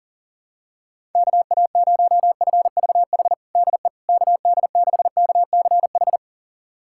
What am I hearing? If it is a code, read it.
Morse code, heard as a keyed tone, about 37 wpm, 700 Hz. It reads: KA0FVH DE KD6KCH
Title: USA Callsign DE Callsign - 6 Characters - 40wpm